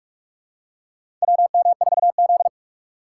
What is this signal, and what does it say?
Morse code, heard as a keyed tone, about 44 wpm, 700 Hz. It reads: WM4Z
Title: USA Call Signs - 4 Characters - 6x Character Spacing - 45wpm